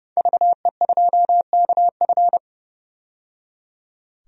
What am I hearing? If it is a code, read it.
Morse code, heard as a keyed tone, about 30 wpm, 700 Hz. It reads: VE2KF
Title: DXCC-100 Call Signs - 5 Characters - 30wpm